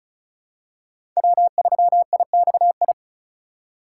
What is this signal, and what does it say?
Morse code, heard as a keyed tone, about 35 wpm, 700 Hz. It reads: W3IXI
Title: USA Call Signs - 5 Characters - 2x Character Spacing - 35wpm